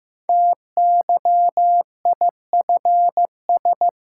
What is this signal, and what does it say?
Morse code, heard as a keyed tone, about 15 wpm, 700 Hz. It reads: TYIFS
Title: Code Groups - 5 Letter:Numbers - 15wpm